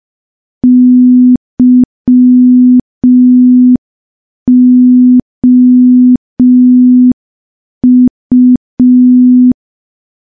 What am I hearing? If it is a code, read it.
Morse code, heard as a keyed tone, about 5 wpm, 250 Hz. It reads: YOU
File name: morse / morse-training-05WPM